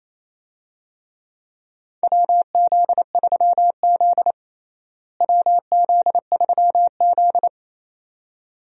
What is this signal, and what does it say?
Morse code, heard as a keyed tone, about 28 wpm, 700 Hz. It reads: WZ3Z WZ3Z
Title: USA Call Signs - 4 Characters - Repeated 2x - 28wpm